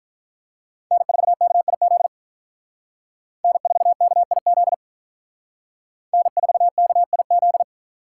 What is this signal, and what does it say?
Morse code, heard as a keyed tone, about 43 wpm, 700 Hz. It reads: N4KIZ N4KIZ N4KIZ
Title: Speed Racing - USA 1x3 Call Signs 35wpm